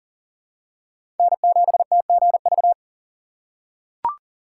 Morse code, heard as a keyed tone, about 40 wpm, 700 Hz. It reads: N7TGV
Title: USA Call Signs - 5 Characters - 3x Character Spacing - 40wpm